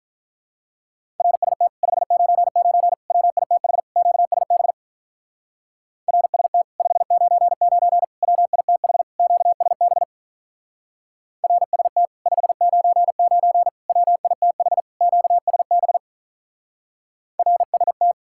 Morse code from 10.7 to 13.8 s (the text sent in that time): RST 599